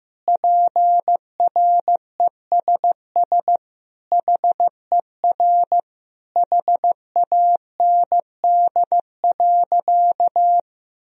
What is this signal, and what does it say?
Morse code, heard as a keyed tone, about 15 wpm, 700 Hz. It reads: PRESS HER HAND.
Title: Sentences from Top 1000 Words - Review Part 1 - 15wpm